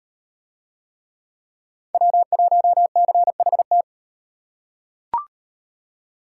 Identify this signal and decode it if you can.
Morse code, heard as a keyed tone, about 38 wpm, 700 Hz. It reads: W1CHT